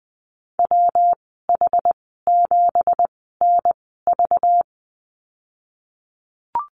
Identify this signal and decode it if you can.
Morse code, heard as a keyed tone, about 20 wpm, spaced as for 15 wpm, 700 Hz. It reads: WH7NV
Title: 231 Lesson 37g - Letter x - Call Signs 20wpm